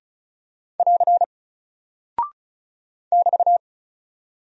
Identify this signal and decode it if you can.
Morse code, heard as a keyed tone, about 36 wpm, 700 Hz. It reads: <AR> <BT>
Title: Single Character (Letter:Number:Punctuation:Prosign) Alt 35wpm